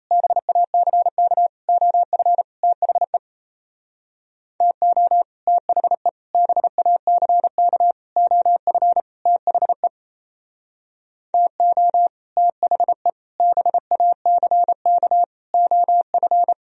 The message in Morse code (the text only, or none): BACK OF THE TO THE BACK OF THE TO THE BACK OF